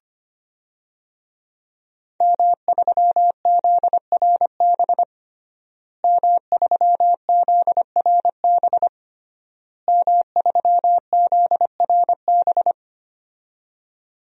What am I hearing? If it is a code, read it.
Morse code, heard as a keyed tone, about 25 wpm, 700 Hz. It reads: M3ZRB M3ZRB M3ZRB
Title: DXCC-100 Call Signs - 5 Characters - Repeated 3x - 25wpm